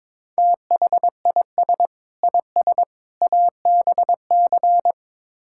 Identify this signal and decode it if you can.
Morse code, heard as a keyed tone, about 21 wpm, 700 Hz. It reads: THIS IS ABC
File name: practice / 22wpm / Build A Phrase - 5-Word Phrases - 22wpm